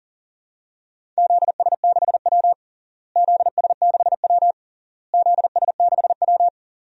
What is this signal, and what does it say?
Morse code, heard as a keyed tone, about 40 wpm, 700 Hz. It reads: ZS6W ZS6W ZS6W
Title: DXCC-100 Call Signs - 4 Characters - Repeated 3x - 40wpm